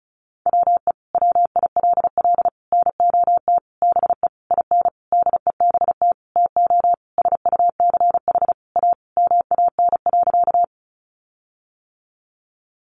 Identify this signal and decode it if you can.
Morse code, heard as a keyed tone, about 34 wpm, 700 Hz. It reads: WE WILL NOT BE IN DEBT TO SUCH A MAN.
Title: Sentences from Top 1300 Words 35wpm